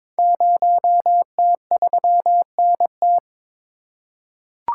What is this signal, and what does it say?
Morse code, heard as a keyed tone, about 22 wpm, 700 Hz. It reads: 0T3NT
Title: Code Groups - 5 Letter:Numbers - 22wpm